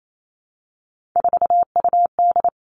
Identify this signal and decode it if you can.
Morse code, heard as a keyed tone, about 28 wpm, 700 Hz. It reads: 4UD
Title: All Cut Numbers - 3-Digit 28wpm